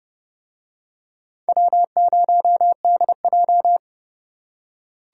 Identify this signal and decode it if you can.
Morse code, heard as a keyed tone, about 29 wpm, 700 Hz. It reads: W0DJ